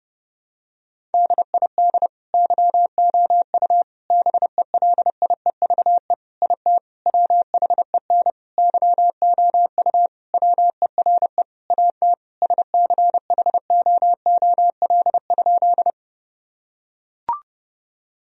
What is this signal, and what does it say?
Morse code, heard as a keyed tone, about 32 wpm, 700 Hz. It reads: DID YOU BELIEVE IT WHEN YOU WERE AT SCHOOL?
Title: Sentences from Top 1000 Words - Review Part 1 - 30wpm